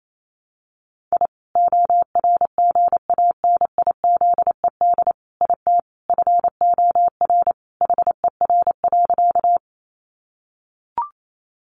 Morse code, heard as a keyed tone, about 28 wpm, 700 Hz. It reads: I ORGANIZED IT FOR HER.
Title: Sentences from Top 1300 Words 28wpm